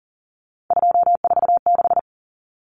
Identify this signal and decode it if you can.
Morse code, heard as a keyed tone, about 40 wpm, 700 Hz. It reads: 246
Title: Common Cut Numbers - 3-Digit 40wpm